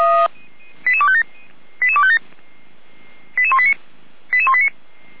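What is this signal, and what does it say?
Five-tone selective call (ZVEI), heard as quick runs of five tones, tones of about 70 ms. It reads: 88257 88257 88189 88189
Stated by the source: ZVEI five-tone selective call 88505 received on SDR.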